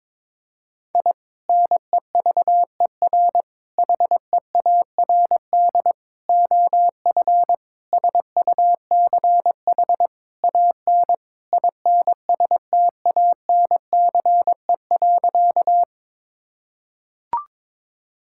Morse code, heard as a keyed tone, about 22 wpm, 700 Hz. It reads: I NEVER HEARD OF SUCH AN INSTANCE.
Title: Sentences from Top 1000 Words - Review Part 5 - 22wpm